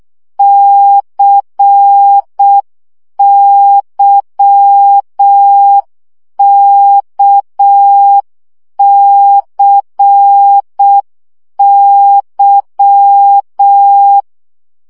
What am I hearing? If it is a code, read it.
Morse code, heard as a keyed tone, about 6 wpm, 800 Hz. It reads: CYKCY
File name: CW gyakorlas k c y 5 baud